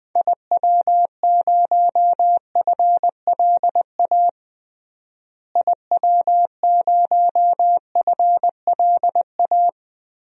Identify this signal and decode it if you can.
Morse code, heard as a keyed tone, about 21 wpm, 700 Hz. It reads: IW0FLA IW0FLA
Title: DXCC-100 Call Signs - 6 Characters - Repeated 3x - 20wpm